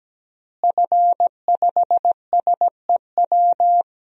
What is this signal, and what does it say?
Morse code, heard as a keyed tone, about 17 wpm, 700 Hz. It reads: F5SEW
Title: DXCC-100 Call Signs - 5 Characters - Repeated 2x - 17wpm